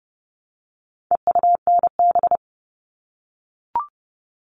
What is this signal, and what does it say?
Morse code, heard as a keyed tone, about 30 wpm, 700 Hz. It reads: EUNB
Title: All Cut Numbers - 4-Digit 30wpm